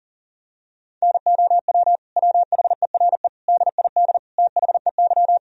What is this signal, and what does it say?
Morse code, heard as a keyed tone, about 40 wpm, 700 Hz. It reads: NOW WHERE DID THEY
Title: Sentences from Top 200 Words and Abbreviations 40wpm